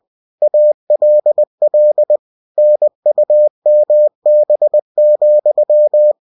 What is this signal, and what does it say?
Morse code, heard as a keyed tone, about 20 wpm, 600 Hz. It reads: ALL NUMB,
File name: cw-books / winnie-the-pooh / winnie-20wpm-page-0089